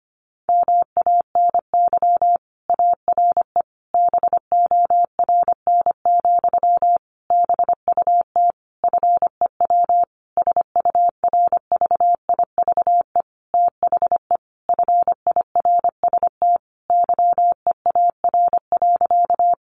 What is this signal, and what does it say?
Morse code, heard as a keyed tone, about 25 wpm, 700 Hz. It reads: MANY ARE BORN, BUT FEW SURVIVE THE FIRST YEAR.
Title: Sentences from Top 1300 Words 25wpm